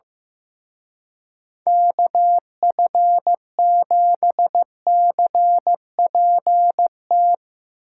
Morse code, heard as a keyed tone, about 17 wpm, 700 Hz. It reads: KF7CPT
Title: Speed Racing - USA Call Signs 15wpm